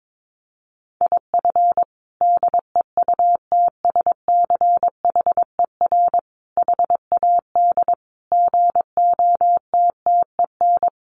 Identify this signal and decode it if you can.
Morse code, heard as a keyed tone, about 22 wpm, 700 Hz. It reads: IF DEUTSCHER HAD GOTTEN
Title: A Sound Of Thunder 22wpm